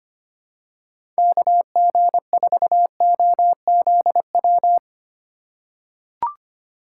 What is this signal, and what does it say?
Morse code, heard as a keyed tone, about 25 wpm, 700 Hz. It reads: KG4OZW
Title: USA Call Signs - 6 Characters - 4x Character Spacing - 25wpm